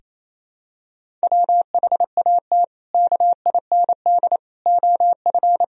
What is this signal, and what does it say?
Morse code, heard as a keyed tone, about 28 wpm, 700 Hz. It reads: WHAT KIND OF
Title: Sentences from Top 1000 Words - Review Part 1 - 28wpm